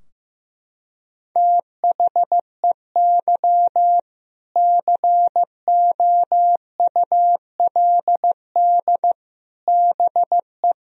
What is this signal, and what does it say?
Morse code, heard as a keyed tone, about 15 wpm, 700 Hz. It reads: THEY COULD BE
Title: Sentences from Top 1000 Words - Review Part 4 - 15wpm